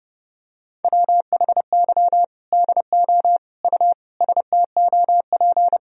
Morse code, heard as a keyed tone, about 30 wpm, 700 Hz. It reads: WHY DO U STOP
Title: Sentences from Top 300 Words and Abbreviations 30wpm